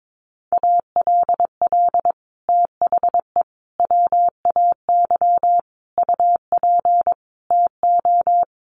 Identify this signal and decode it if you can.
Morse code, heard as a keyed tone, about 22 wpm, 700 Hz. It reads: ALL THE WAY UP TO
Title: Sets of 5 Words Encore 22wpm